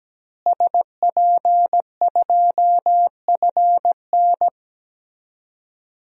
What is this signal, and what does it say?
Morse code, heard as a keyed tone, about 17 wpm, 700 Hz. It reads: SP2FN
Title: DXCC-100 Call Signs - 5 Characters - Repeated 2x - 17wpm